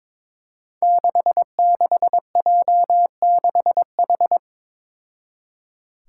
Morse code, heard as a keyed tone, about 22 wpm, 700 Hz. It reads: 66J6H
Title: Five Challenging Letter-Number Combinations 22wpm